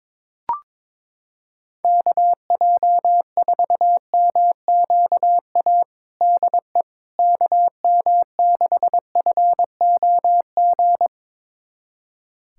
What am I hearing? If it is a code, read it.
Morse code, heard as a keyed tone, about 21 wpm, 700 Hz. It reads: KJ4MQA DE KM6FOG
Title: USA Callsign DE Callsign - 6 Characters - 22wpm